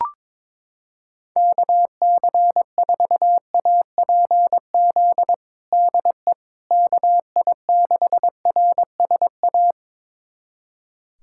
Morse code, heard as a keyed tone, about 22 wpm, 700 Hz. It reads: KC4APZ DE KI6RSA
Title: USA Callsign DE Callsign - 6 Characters - 22wpm